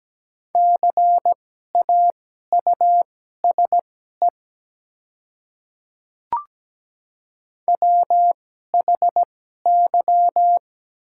Morse code, heard as a keyed tone, about 17 wpm, spaced as for 12 wpm, 700 Hz. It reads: CAUSE WHY
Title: 2x Character Spacing - Top 200 Words - 17wpm